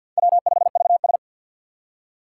Morse code, heard as a keyed tone, about 47 wpm, 700 Hz. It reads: W5VS
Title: USA Call Signs - 4 Characters - 6x Character Spacing - 50wpm